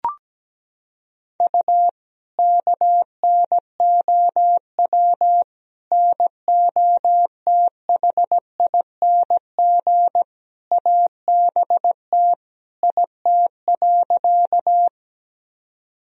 Morse code, heard as a keyed tone, about 17 wpm, 700 Hz. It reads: U KNOW NOTHING ABT IT.
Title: Sentences from Top 500 Words and Abbreviations 17wpm